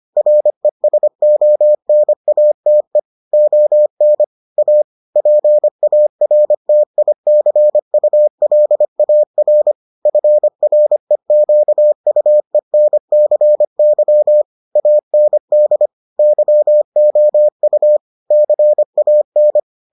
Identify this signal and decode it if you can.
Morse code, heard as a keyed tone, about 25 wpm, 600 Hz. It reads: RESONATE ON A PARTICULAR FREQUENCY AND YOU CAN